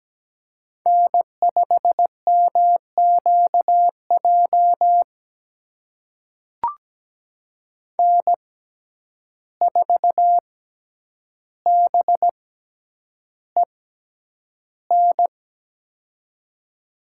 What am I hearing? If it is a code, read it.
Morse code, heard as a keyed tone, about 17 wpm, spaced as for 9 wpm, 700 Hz. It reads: N5MQJ N4BEN
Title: USA Call Signs - 5 Characters - 6x Character Spacing - 17wpm